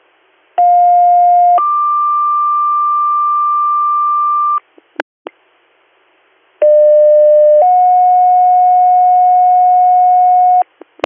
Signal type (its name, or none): two-tone page